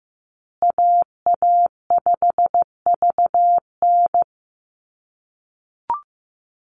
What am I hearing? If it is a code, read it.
Morse code, heard as a keyed tone, about 15 wpm, 700 Hz. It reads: AA5VN
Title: USA Call Signs Repeated 3x 15wpm